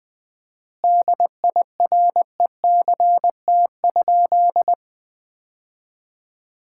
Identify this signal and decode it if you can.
Morse code, heard as a keyed tone, about 20 wpm, 700 Hz. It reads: DIRECT?
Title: Common QSO Phrases - 4x Word-Spacing - 20wpm